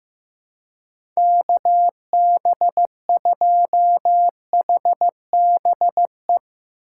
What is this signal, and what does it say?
Morse code, heard as a keyed tone, about 15 wpm, 700 Hz. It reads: KB2HBE
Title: USA Call Signs - 6 Characters - 4x Character Spacing - 15wpm